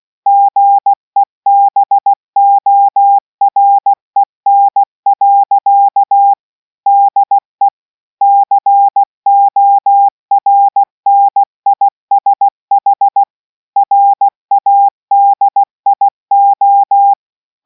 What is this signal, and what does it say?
Morse code, heard as a keyed tone, about 16 wpm, 800 Hz. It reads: GEBOREN. DE CORNISH RADIO